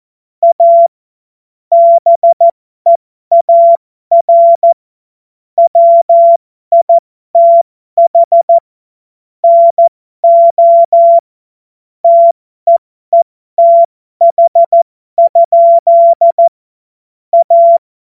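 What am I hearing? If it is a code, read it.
Morse code, heard as a keyed tone, about 14 wpm, spaced as for 12 wpm, 680 Hz. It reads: A BEAR WITH NO TEETH? A
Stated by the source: Morse code joke.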